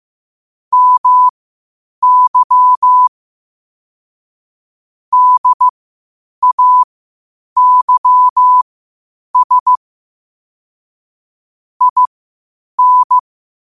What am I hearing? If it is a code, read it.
Morse code, heard as a keyed tone, about 15 wpm, spaced as for 8 wpm, 1000 Hz. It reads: MY DAYS IN
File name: qotdfiles / 8wpm